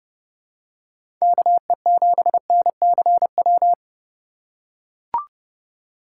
Morse code, heard as a keyed tone, about 31 wpm, 700 Hz. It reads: KE7NCW
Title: USA Call Signs - 6 Characters - 3x Character Spacing - 30wpm